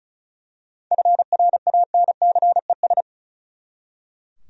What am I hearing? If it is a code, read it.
Morse code, heard as a keyed tone, about 36 wpm, 700 Hz. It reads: FRANCES
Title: USA Common Names - 35wpm